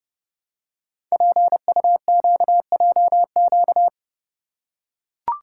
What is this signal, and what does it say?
Morse code, heard as a keyed tone, about 28 wpm, 690 Hz. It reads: PUQJQ
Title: Code Groups - 5 Letters - 30wpm